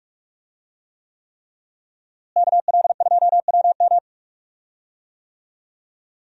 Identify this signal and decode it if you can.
Morse code, heard as a keyed tone, about 46 wpm, 700 Hz. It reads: KL2WM
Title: USA Call Signs - 5 Characters - Repeated 3x - 45wpm